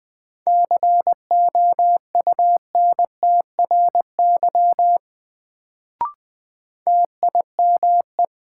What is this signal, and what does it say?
Morse code, heard as a keyed tone, about 20 wpm, 700 Hz. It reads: COUNTRY TIME
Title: Sets of 2 Words - New Word Drill - 20wpm